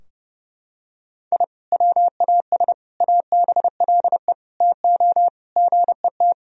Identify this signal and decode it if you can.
Morse code, heard as a keyed tone, about 32 wpm, 700 Hz. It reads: I WAS ABLE TO GET
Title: Sentences from Top 1000 Words - Review Part 2 - 30wpm